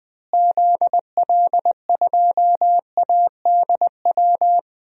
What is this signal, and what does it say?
Morse code, heard as a keyed tone, about 20 wpm, 700 Hz. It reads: ZL2ADW
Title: DXCC-100 Call Signs - 6 Characters - Repeated 3x - 20wpm